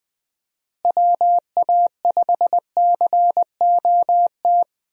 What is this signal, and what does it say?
Morse code, heard as a keyed tone, about 20 wpm, 700 Hz. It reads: WA5COT
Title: USA Call Signs - 6 Characters - Repeated 3x - 20wpm